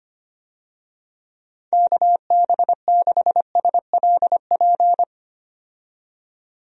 Morse code, heard as a keyed tone, about 25 wpm, 700 Hz. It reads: KB6SLP
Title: USA Call Signs - 6 Characters - 25wpm